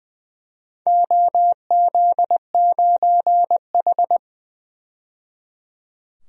Morse code, heard as a keyed tone, about 20 wpm, 700 Hz. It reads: OZ9H
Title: DXCC-100 Call Signs - 4 Characters - 20wpm